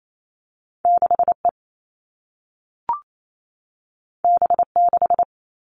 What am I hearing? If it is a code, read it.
Morse code, heard as a keyed tone, about 28 wpm, 700 Hz. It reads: 6E B6
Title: All Cut Numbers - 2-Digit 28wpm